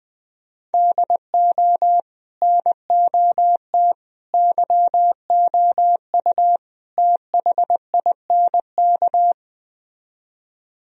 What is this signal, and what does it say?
Morse code, heard as a keyed tone, about 20 wpm, 700 Hz. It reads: DO NOT YOU THINK